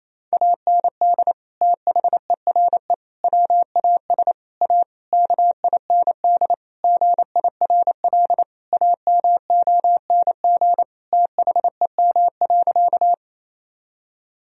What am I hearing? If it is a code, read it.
Morse code, heard as a keyed tone, about 28 wpm, 700 Hz. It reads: AND THERE WAS A KIND GIRL AMONG THEM.
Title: Sentences from Top 1000 Words - Review Part 3 - 28wpm